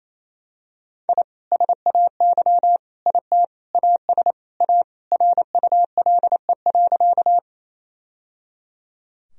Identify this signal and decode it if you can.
Morse code, heard as a keyed tone, about 28 wpm, 700 Hz. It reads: I SAY IT AS A RULE.
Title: Sentences from Top 1000 Words - Review Part 2 - 28wpm